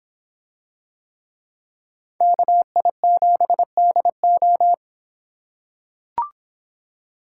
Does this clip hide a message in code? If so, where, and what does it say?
Yes: Morse code from 2.2 to 4.7 s, heard as a keyed tone, about 24 wpm, 700 Hz. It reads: KI7DO